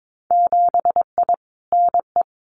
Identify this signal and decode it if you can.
Morse code, heard as a keyed tone, about 22 wpm, 700 Hz. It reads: 7I NE
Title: Winter Field Day Exchanges Encore 22wpm